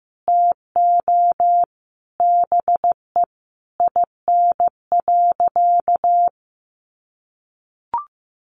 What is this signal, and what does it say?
Morse code, heard as a keyed tone, about 15 wpm, 700 Hz. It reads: TO BE IN.
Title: Sentences from Top 1200 Words 15wpm